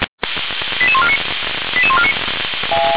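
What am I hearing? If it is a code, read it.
Five-tone selective call (ZVEI), heard as quick runs of five tones, tones of about 70 ms. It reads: 88149 88149
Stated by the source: Received on an SDR.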